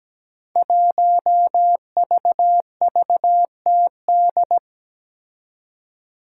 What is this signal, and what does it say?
Morse code, heard as a keyed tone, about 17 wpm, 700 Hz. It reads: 1VVTD
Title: Five Challenging Letter-Number Combinations 17wpm